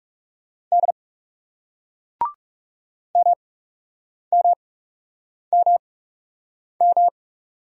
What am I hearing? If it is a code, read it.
Morse code, heard as a keyed tone, about 38 wpm, 700 Hz. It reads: D M M M M